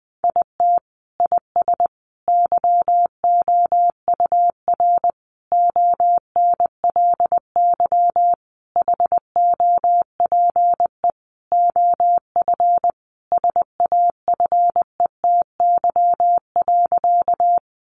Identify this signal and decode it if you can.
Morse code, heard as a keyed tone, about 20 wpm, 700 Hz. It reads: IT IS YOUR ONLY HOPE OF SAFETY.
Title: Sentences from Top 1000 Words 20wpm